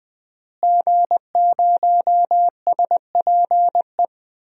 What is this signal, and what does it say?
Morse code, heard as a keyed tone, about 20 wpm, 700 Hz. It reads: G0SPE
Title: DXCC-100 Call Signs - 5 Characters - Repeated 3x - 20wpm